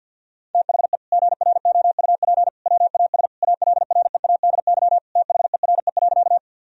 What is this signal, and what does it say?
Morse code, heard as a keyed tone, about 50 wpm, 700 Hz. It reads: THE GROUP WAS ALREADY THERE.
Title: Sentences from Top 1000 Words - Review Part 1 - 50wpm